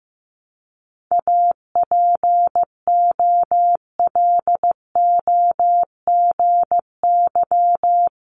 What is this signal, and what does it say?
Morse code, heard as a keyed tone, about 15 wpm, 700 Hz. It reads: APOLOGY
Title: Multi-letter Phonemes 15wpm